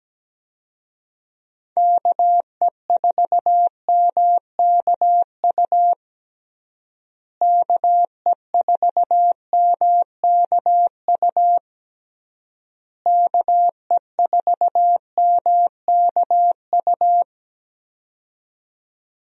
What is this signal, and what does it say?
Morse code, heard as a keyed tone, about 17 wpm, 700 Hz. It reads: KE4MKU KE4MKU KE4MKU
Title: USA Call Signs - 6 Characters - Repeated 3x - 17wpm